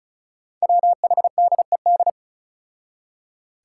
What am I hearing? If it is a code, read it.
Morse code, heard as a keyed tone, about 36 wpm, 700 Hz. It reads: WHDED
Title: Five Challenging Letter-Number Combinations 35wpm